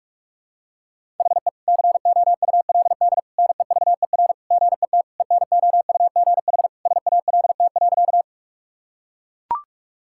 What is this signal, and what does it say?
Morse code, heard as a keyed tone, about 41 wpm, 700 Hz. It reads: HE COULD NEVER GET ENOUGH SALT.
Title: Sentences from Top 1000 Words - Review Part 5 - 45wpm